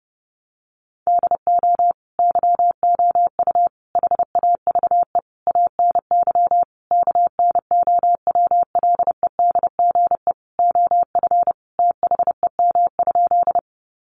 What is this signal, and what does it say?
Morse code, heard as a keyed tone, about 30 wpm, 700 Hz. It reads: DO YOU HAVE ANY KNOWLEDGE OF THEM?
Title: Sentences from Top 600 Words 30wpm